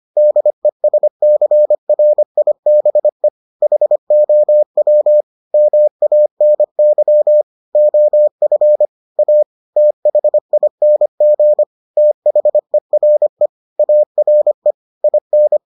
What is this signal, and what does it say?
Morse code, heard as a keyed tone, about 25 wpm, 600 Hz. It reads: DESCRIBE HOW MANY OF A THING THERE ARE IN